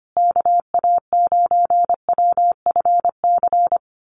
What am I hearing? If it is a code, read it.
Morse code, heard as a keyed tone, about 25 wpm, 700 Hz. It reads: KA9WFC
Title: USA Call Signs Repeated 2x 25wpm